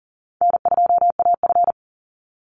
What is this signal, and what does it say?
Morse code, heard as a keyed tone, about 40 wpm, 700 Hz. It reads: N2AF
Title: NA SOTA Chaser Callsigns 40wpm